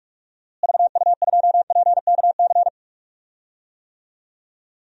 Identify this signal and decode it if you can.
Morse code, heard as a keyed tone, about 46 wpm, 700 Hz. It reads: VU2PKC
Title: DXCC-100 Call Signs - 6 Characters - Repeated 3x - 45wpm